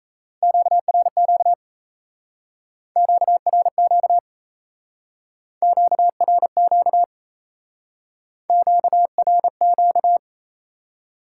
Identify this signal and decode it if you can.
Morse code, heard as a keyed tone, about 34 wpm, 700 Hz. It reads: QRQ QRQ QRQ QRQ
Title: Speed-Racing Top 90 QSO Elements 28wpm